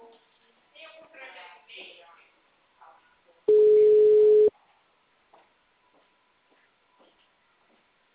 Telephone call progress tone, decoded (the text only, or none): ringback tone